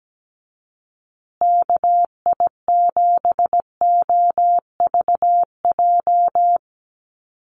Morse code, heard as a keyed tone, about 17 wpm, 700 Hz. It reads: KI7OVJ
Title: USA Call Signs Repeated 3x 17wpm